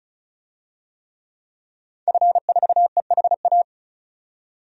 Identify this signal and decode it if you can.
Morse code, heard as a keyed tone, about 34 wpm, 700 Hz. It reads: F4EHA
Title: DXCC-100 Call Signs - 5 Characters - 35wpm